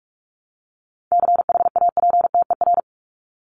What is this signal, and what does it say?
Morse code, heard as a keyed tone, about 44 wpm, 700 Hz. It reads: CHAPTER
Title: Top 1751-2000 Words 45wpm